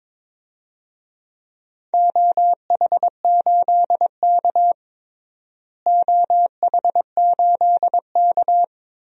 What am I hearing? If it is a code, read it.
Morse code, heard as a keyed tone, about 22 wpm, 700 Hz. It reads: OH8K OH8K
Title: DXCC-100 Call Signs - 4 Characters - Repeated 3x - 22wpm